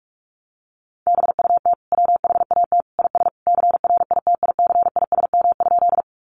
Morse code, heard as a keyed tone, about 44 wpm, 700 Hz. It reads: BUT WHAT IS CRITICISM?
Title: Sentences from Top 1300 Words 45wpm